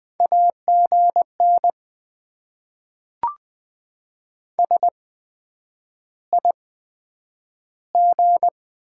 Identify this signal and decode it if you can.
Morse code, heard as a keyed tone, about 20 wpm, spaced as for 9 wpm, 700 Hz. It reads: AGN SIG
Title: Top 90 QSO Elements - 8x Character Spacing - 20wpm